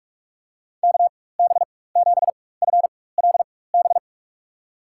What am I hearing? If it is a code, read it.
Morse code, heard as a keyed tone, about 46 wpm, spaced as for 21 wpm, 700 Hz. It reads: KB7FLB
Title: USA Call Signs - 6 Characters - 4x Character Spacing - 45wpm